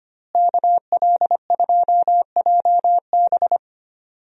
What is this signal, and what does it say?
Morse code, heard as a keyed tone, about 25 wpm, 700 Hz. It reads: KL2JB
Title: USA Call Signs - 5 Characters - Repeated 3x - 25wpm